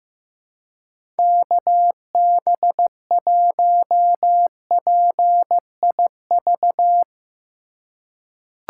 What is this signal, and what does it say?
Morse code, heard as a keyed tone, about 16 wpm, 700 Hz. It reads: KB1PIV
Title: USA Call Signs - 3x Character Spacing - 15wpm